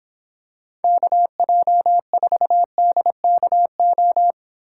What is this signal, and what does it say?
Morse code, heard as a keyed tone, about 26 wpm, 700 Hz. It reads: KJ4DKO